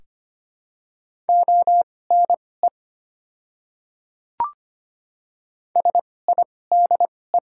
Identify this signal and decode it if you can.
Morse code, heard as a keyed tone, about 25 wpm, spaced as for 18 wpm, 700 Hz. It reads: ONE SIDE
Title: 2x Character Spacing - Top 100 Words - 25wpm